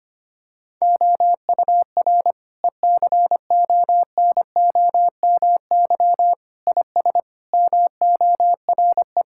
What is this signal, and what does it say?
Morse code, heard as a keyed tone, about 25 wpm, 700 Hz. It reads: OUR ECONOMY IS MORE
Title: Sentences from Top 1000 Words - Review Part 6 - 25wpm